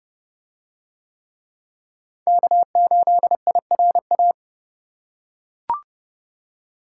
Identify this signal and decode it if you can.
Morse code, heard as a keyed tone, about 28 wpm, 700 Hz. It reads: K8IRA